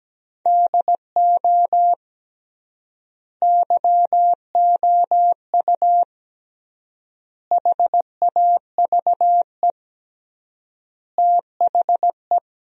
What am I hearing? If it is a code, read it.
Morse code, heard as a keyed tone, about 17 wpm, 700 Hz. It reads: DO YOU HAVE THE
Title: 3x Word-Spacing - Sentences from Top 300 Words - 17wpm